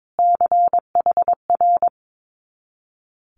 Morse code, heard as a keyed tone, about 22 wpm, 700 Hz. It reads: CHR
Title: Multi-letter Phonemes Repeated 3x 22wpm